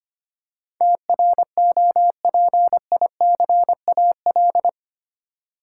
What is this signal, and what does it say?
Morse code, heard as a keyed tone, about 25 wpm, 700 Hz. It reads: TROPICAL
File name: Eight Letter Words - 25wpm